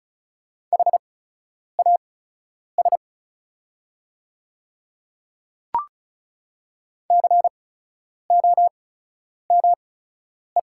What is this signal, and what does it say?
Morse code, heard as a keyed tone, about 33 wpm, spaced as for 9 wpm, 700 Hz. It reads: HAS COME
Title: 8x Character Spacing - Top 100 Words - 35wpm